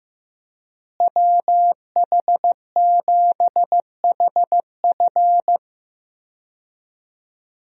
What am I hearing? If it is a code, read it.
Morse code, heard as a keyed tone, about 15 wpm, 700 Hz. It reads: WH7HF
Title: USA Call Signs - 5 Characters - Repeated 3x - 15wpm